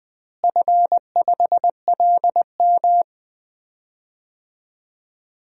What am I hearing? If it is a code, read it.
Morse code, heard as a keyed tone, about 20 wpm, 700 Hz. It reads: F5LM
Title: DXCC-100 Call Signs - 4 Characters - Repeated 3x - 20wpm